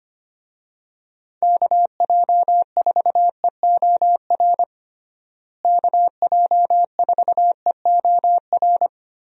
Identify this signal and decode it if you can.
Morse code, heard as a keyed tone, about 25 wpm, 700 Hz. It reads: KJ4EOR KJ4EOR
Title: USA Call Signs - 6 Characters - Repeated 2x - 25wpm